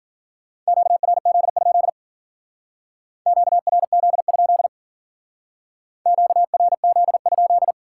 Morse code, heard as a keyed tone, about 49 wpm, 700 Hz. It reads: QRZ? QRZ? QRZ?
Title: Speed-Racing Common QSO Phrases - 40wpm